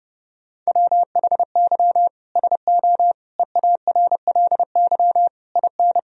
Morse code, heard as a keyed tone, about 32 wpm, 700 Hz. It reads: WHY SO EARLY IN
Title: Sentences from Top 1000 Words - Review Part 2 - 30wpm